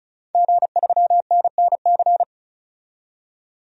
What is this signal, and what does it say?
Morse code, heard as a keyed tone, about 34 wpm, 700 Hz. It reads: G3NNC
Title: DXCC-100 Call Signs - 5 Characters - 35wpm